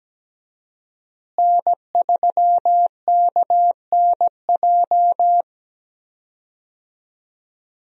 Morse code, heard as a keyed tone, about 17 wpm, 700 Hz. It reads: N3KNJ
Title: USA Call Signs - 5 Characters - Repeated 2x - 17wpm